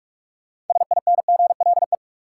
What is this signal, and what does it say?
Morse code, heard as a keyed tone, about 46 wpm, 700 Hz. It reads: SINGLE
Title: Sentences from Top 700 Words - New Word Drill - 45wpm